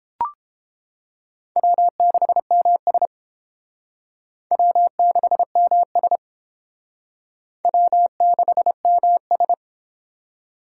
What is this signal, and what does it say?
Morse code, heard as a keyed tone, about 31 wpm, 700 Hz. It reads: W6MS W6MS W6MS